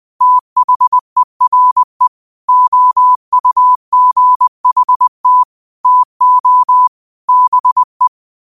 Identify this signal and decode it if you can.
Morse code, heard as a keyed tone, about 20 wpm, 1000 Hz. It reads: THERE OUGHT TO BE
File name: qotdfiles / 20wpm